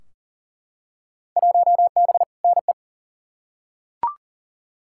Morse code, heard as a keyed tone, about 37 wpm, 700 Hz. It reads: 1B NE
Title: Field Day Exchanges Encore - 40wpm